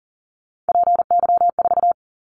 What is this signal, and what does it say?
Morse code, heard as a keyed tone, about 40 wpm, 700 Hz. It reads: PY4
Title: Callsign Prefixes for Top 100 DXCC - 40wpm